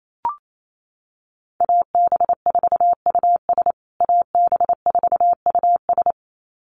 Morse code, heard as a keyed tone, about 28 wpm, 700 Hz. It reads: AB4US AB4US
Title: USA Call Signs Repeated 2x 28wpm